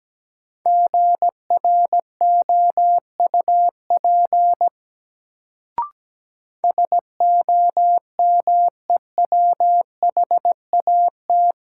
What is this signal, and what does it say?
Morse code, heard as a keyed tone, about 17 wpm, 700 Hz. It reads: GROUP SOMEWHAT
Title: Sentences from Top 1000 Words - Review Part 1 - New Word Drill - 17wpm